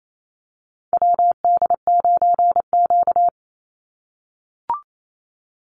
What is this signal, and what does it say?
Morse code, heard as a keyed tone, about 28 wpm, 700 Hz. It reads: WD9Q
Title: Call Signs - 28wpm